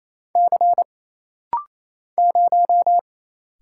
Morse code, heard as a keyed tone, about 28 wpm, 700 Hz. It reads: C 0
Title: Single Letter-Number Alt - Rapid-Fire 28wpm